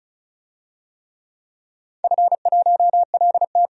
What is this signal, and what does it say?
Morse code, heard as a keyed tone, about 36 wpm, 700 Hz. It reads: F1LT
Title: DXCC-100 Call Signs - 4 Characters - 35wpm